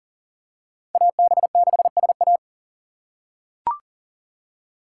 Morse code, heard as a keyed tone, about 40 wpm, 700 Hz. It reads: AB6SA
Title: USA Call Signs - 5 Characters - 2x Character Spacing - 40wpm